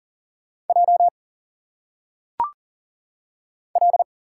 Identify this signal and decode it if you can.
Morse code, heard as a keyed tone, about 40 wpm, 700 Hz. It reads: J L
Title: Single Letters - Phonetic Pronunciation 40wpm